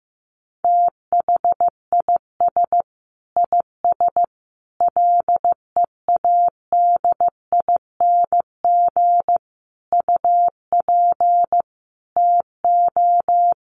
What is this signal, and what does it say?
Morse code, heard as a keyed tone, about 15 wpm, 700 Hz. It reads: THIS IS LEADING UP TO
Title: Sentences from Top 1100 Words 15wpm